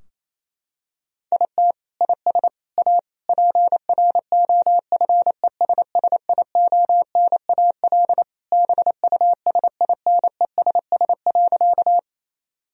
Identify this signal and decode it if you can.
Morse code, heard as a keyed tone, about 28 wpm, 700 Hz. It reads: IT IS A PROFESSIONAL BUSINESS.
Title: Sentences from Top 1000 Words - Review Part 4 - 28wpm